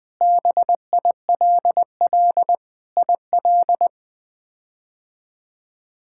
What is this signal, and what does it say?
Morse code, heard as a keyed tone, about 19 wpm, 700 Hz. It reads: BILL IL
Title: K1USN SST QSO Practice - 20wpm